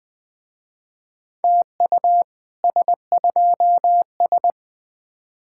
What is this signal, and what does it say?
Morse code, heard as a keyed tone, about 21 wpm, 700 Hz. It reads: TU S2S
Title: Speed-Racing Common QSO Phrases - 17wpm